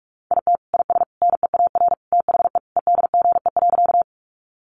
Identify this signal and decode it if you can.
Morse code, heard as a keyed tone, about 44 wpm, 700 Hz. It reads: IT IS NEAR THE EDGE.
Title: Additional Easy Phrases 45wpm